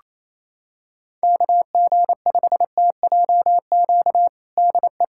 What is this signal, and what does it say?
Morse code, heard as a keyed tone, about 27 wpm, 700 Hz. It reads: KG5TJQ DE
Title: USA Callsign DE Callsign - 6 Characters - 28wpm